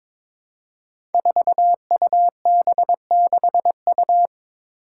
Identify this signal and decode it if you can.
Morse code, heard as a keyed tone, about 22 wpm, 700 Hz. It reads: 4UB6U
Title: Five Challenging Letter-Number Combinations 22wpm